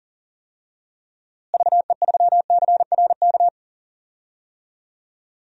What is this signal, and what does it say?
Morse code, heard as a keyed tone, about 40 wpm, 700 Hz. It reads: VE3CRK
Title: DXCC-100 Call Signs - 6 Characters - Repeated 3x - 40wpm